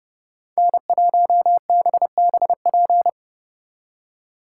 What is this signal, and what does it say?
Morse code, heard as a keyed tone, about 35 wpm, 700 Hz. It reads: N1BBP
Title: Speed Racing - USA 1x3 Call Signs 30wpm